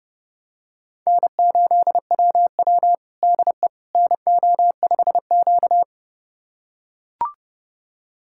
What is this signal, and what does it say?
Morse code, heard as a keyed tone, about 32 wpm, 700 Hz. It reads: N8WW DE NO5Q
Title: USA Callsign DE Callsign - 4 Characters - 30wpm